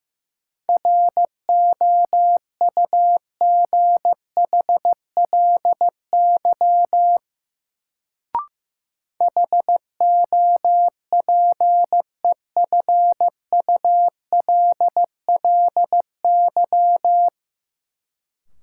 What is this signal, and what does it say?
Morse code, heard as a keyed tone, about 15 wpm, 700 Hz. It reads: ROUGHLY HOPEFULLY
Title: Sentences from Top 1500 Words - New Word Drill - 15wpm